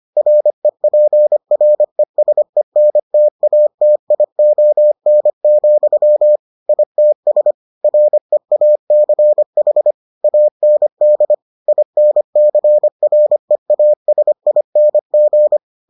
Morse code, heard as a keyed tone, about 25 wpm, 600 Hz. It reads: REPRESENTATION, ITS REACH AND INCREASING